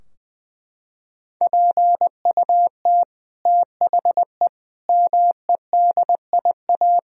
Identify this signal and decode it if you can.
Morse code, heard as a keyed tone, about 19 wpm, 700 Hz. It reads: PUT THE MEDIA
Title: Sentences from Top 1000 Words - Review Part 4 - 20wpm